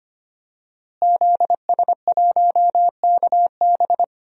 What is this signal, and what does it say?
Morse code, heard as a keyed tone, about 25 wpm, 700 Hz. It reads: ZS1KB
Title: DXCC-100 Call Signs - 5 Characters - Repeated 2x - 25wpm